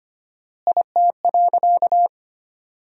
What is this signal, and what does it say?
Morse code, heard as a keyed tone, about 25 wpm, 700 Hz. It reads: IT.
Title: 4x Word-Spacing - Sentences from Top 200 Words - 25wpm